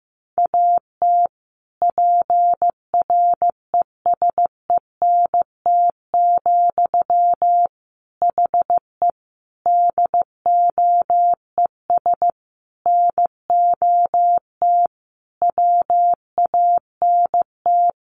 Morse code, heard as a keyed tone, about 14 wpm, 700 Hz. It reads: AT PRESENT, HE DOES NOT WANT
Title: Sentences from Top 800 Words 15wpm